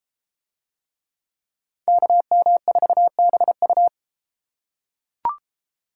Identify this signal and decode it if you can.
Morse code, heard as a keyed tone, about 30 wpm, 700 Hz. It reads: KM4BU